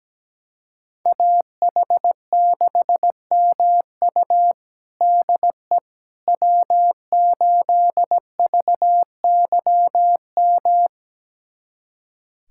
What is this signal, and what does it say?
Morse code, heard as a keyed tone, about 17 wpm, 700 Hz. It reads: AH6MU DE W8VYM
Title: USA Callsign DE Callsign - 5 Characters - 17wpm